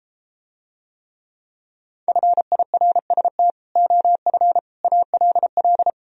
Morse code, heard as a keyed tone, about 33 wpm, 700 Hz. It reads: FIRST OF ALL